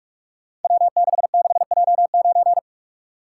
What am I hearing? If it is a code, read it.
Morse code, heard as a keyed tone, about 44 wpm, 700 Hz. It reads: W66J9
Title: Code Groups - 5 Letter:Numbers - 45wpm